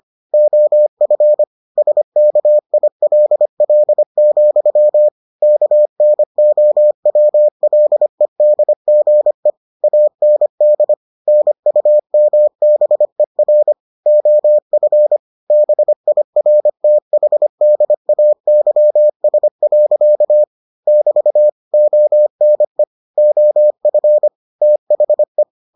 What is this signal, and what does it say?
Morse code, heard as a keyed tone, about 25 wpm, 600 Hz. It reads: OF SKILL, KNOWLEDGE AND NUMBER OF BIRTHDAYS. <BT> ONE OF THE